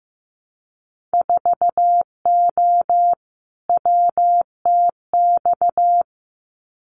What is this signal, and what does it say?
Morse code, heard as a keyed tone, about 15 wpm, 700 Hz. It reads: 4O WTX
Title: Winter Field Day Exchanges Encore 15wpm